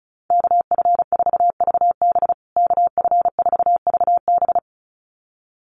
Morse code, heard as a keyed tone, about 34 wpm, 700 Hz. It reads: KF4VB KF4VB
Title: USA Call Signs Repeated 2x 35wpm